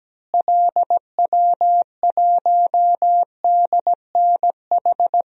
Morse code, heard as a keyed tone, about 17 wpm, 700 Hz. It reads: LW1DNH
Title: DXCC-100 Call Signs - 6 Characters - Repeated 2x - 17wpm